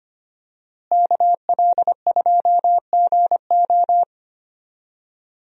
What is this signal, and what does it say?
Morse code, heard as a keyed tone, about 25 wpm, 700 Hz. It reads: KL2GO
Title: USA Call Signs - 5 Characters - Repeated 2x - 25wpm